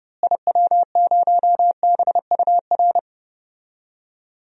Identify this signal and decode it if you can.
Morse code, heard as a keyed tone, about 30 wpm, 700 Hz. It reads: IW0BUR
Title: DXCC-100 Call Signs - 6 Characters - Repeated 3x - 30wpm